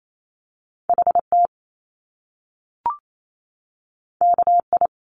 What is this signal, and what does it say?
Morse code, heard as a keyed tone, about 28 wpm, 700 Hz. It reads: H T K I
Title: Top Two Letter Combinations in Words 28wpm